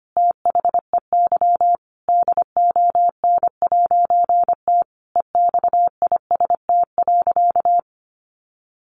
Morse code, heard as a keyed tone, about 25 wpm, 700 Hz. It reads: THEY DON'T EXIST.
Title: A Sound Of Thunder 25wpm